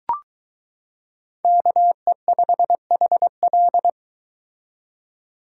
Morse code, heard as a keyed tone, about 23 wpm, 700 Hz. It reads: KE5HL